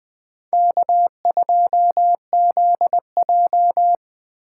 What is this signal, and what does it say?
Morse code, heard as a keyed tone, about 20 wpm, 700 Hz. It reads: K2ZJ